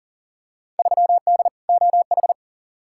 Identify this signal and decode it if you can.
Morse code, heard as a keyed tone, about 40 wpm, 700 Hz. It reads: 3D OH
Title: Field Day Exchanges - 40wpm